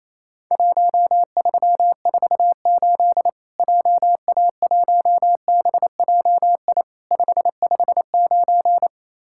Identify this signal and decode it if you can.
Morse code, heard as a keyed tone, about 28 wpm, 700 Hz. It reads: 1348 JA1BJI 559
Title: Worked All Europe DX Contest - QTC Exchange - 28wpm